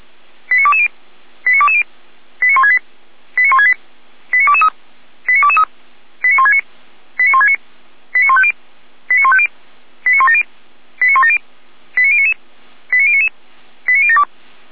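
Five-tone selective call (ZVEI), heard as quick runs of five tones, tones of about 80 ms. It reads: 89220 89220 89168 89168 89223 89223 89169 89169 89150 89150 89170 89170 89099 89099 89072
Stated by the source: ZVEI five-tone call 89167 received by SDR, logged on the audio server of feuerwehr-orlamuende.org.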